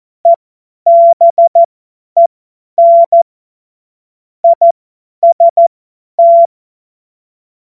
Morse code, heard as a keyed tone, about 14 wpm, spaced as for 10 wpm, 680 Hz. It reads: EBEN IST
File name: Morsekurs / Witze / Witz 04. April 2023 50 F70